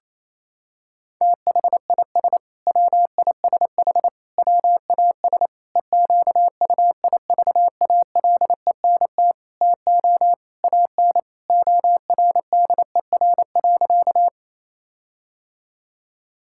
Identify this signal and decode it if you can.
Morse code, heard as a keyed tone, about 28 wpm, 700 Hz. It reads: THIS WISH WAS EQUIVALENT TO AN ORDER.
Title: Sentences from Top 1600 Words 28wpm